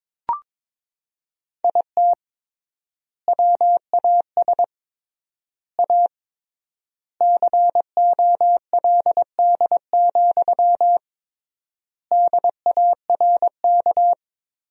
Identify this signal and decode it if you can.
Morse code, heard as a keyed tone, about 21 wpm, 700 Hz. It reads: IT WAS A COLD, DARK
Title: 3x Word-Spacing - Sentences from Top 500 Words - 22wpm